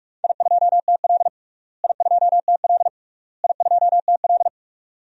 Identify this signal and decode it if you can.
Morse code, heard as a keyed tone, about 46 wpm, 700 Hz. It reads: I2TL I2TL I2TL
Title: DXCC-100 Call Signs - 4 Characters - Repeated 3x - 45wpm